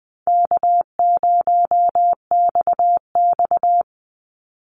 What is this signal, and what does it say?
Morse code, heard as a keyed tone, about 20 wpm, 700 Hz. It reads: K0XX
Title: Easy Call Signs - 20wpm